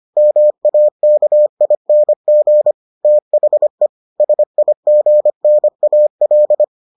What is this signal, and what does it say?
Morse code, heard as a keyed tone, about 25 wpm, 600 Hz. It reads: MAKING THE SIGNAL